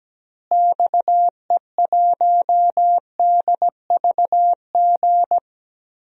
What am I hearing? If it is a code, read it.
Morse code, heard as a keyed tone, about 17 wpm, 700 Hz. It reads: XE1DVG
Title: DXCC-100 Call Signs - 6 Characters - Repeated 3x - 17wpm